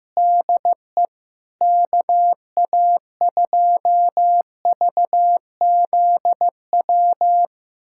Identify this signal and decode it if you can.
Morse code, heard as a keyed tone, about 15 wpm, 700 Hz. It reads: DE KA2VZW
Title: USA Callsign DE Callsign 15wpm